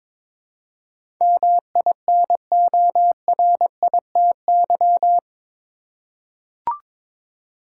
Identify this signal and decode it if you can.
Morse code, heard as a keyed tone, about 22 wpm, 700 Hz. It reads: MINORITY
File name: practice / 22wpm / Eight Letter Words - 22wpm